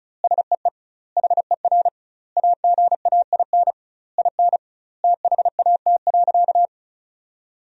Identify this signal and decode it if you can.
Morse code, heard as a keyed tone, about 36 wpm, 700 Hz. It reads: SEE HER AGAIN IN THAT.
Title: 2x Word-Spacing - Sentences from Top 200 Words - 35wpm